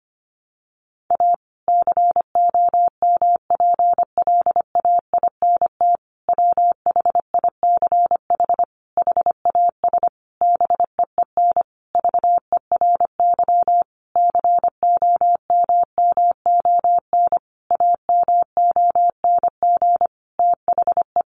Morse code, heard as a keyed tone, about 25 wpm, 700 Hz. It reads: A COMPLAINT WHICH HAS BEEN VERY COMMON AMONG THE
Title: Sentences from Top 1400 Words 25wpm